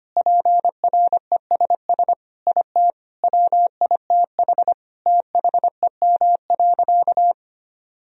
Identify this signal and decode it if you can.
Morse code, heard as a keyed tone, about 25 wpm, 700 Hz. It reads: PRESS IT WITH THEM.
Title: Sentences from Top 1000 Words - Review Part 1 - 25wpm